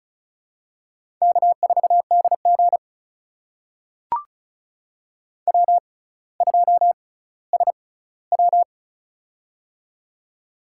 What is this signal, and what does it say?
Morse code, heard as a keyed tone, about 36 wpm, spaced as for 18 wpm, 700 Hz. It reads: K4DG W2SW
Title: USA Call Signs - 6x Character Spacing - 35wpm